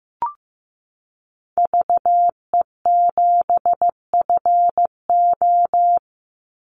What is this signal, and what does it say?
Morse code, heard as a keyed tone, about 15 wpm, 700 Hz. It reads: VE7FO
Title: Call Signs Repeated 3x - 15wpm